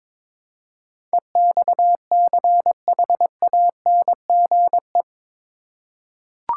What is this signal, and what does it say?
Morse code, heard as a keyed tone, about 23 wpm, 700 Hz. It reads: EXCHANGE
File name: practice / 22wpm / Eight Letter Words - 22wpm